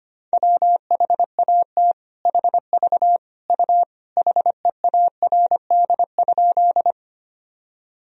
Morse code, heard as a keyed tone, about 25 wpm, 700 Hz. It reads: WHAT HV U HEARD?
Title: Sentences from Top 400 Words and Abbreviations 25wpm